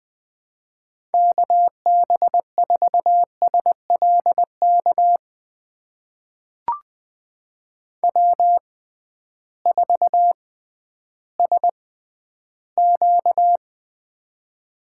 Morse code, heard as a keyed tone, about 20 wpm, spaced as for 12 wpm, 700 Hz. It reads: KB4SLK W4SQ
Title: USA Call Signs - 6x Character Spacing - 20wpm